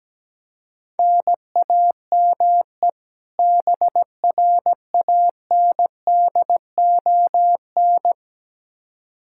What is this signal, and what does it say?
Morse code, heard as a keyed tone, about 17 wpm, 700 Hz. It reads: NAME BRANDON
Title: K1USN SST QSO Practice 17wpm